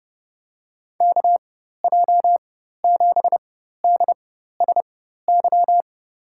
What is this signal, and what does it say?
Morse code, heard as a keyed tone, about 30 wpm, spaced as for 14 wpm, 700 Hz. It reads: KJ7DSY
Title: USA Call Signs - 6 Characters - 4x Character Spacing - 30wpm